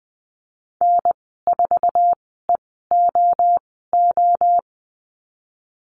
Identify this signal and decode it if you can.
Morse code, heard as a keyed tone, about 20 wpm, spaced as for 15 wpm, 700 Hz. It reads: N4EOO
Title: 048 Lesson 9g - Number 4 - Call Signs 20wpm